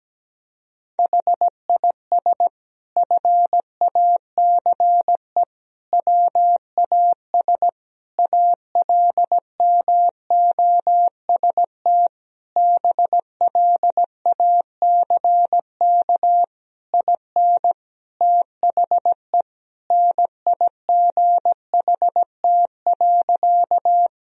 Morse code, heard as a keyed tone, about 17 wpm, 700 Hz. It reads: HIS FACE WAS ALMOST BLACK IN THE NIGHT.
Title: Sentences from Top 1000 Words - Review Part 4 - 17wpm